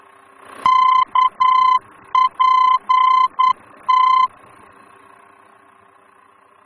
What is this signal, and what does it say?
Morse code, heard as a keyed tone, about 9 wpm, 1030 Hz. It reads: KPT
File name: germany / mx / KPT-D-VOR